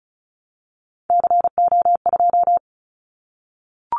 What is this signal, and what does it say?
Morse code, heard as a keyed tone, about 36 wpm, 700 Hz. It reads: CO2
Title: Callsign Prefixes for Top 100 DXCC - 35wpm